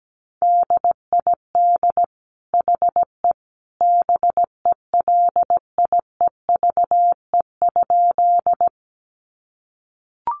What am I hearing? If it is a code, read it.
Morse code, heard as a keyed tone, about 17 wpm, 700 Hz. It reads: DID HE BELIEVE?
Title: Sentences from Top 1000 Words 17wpm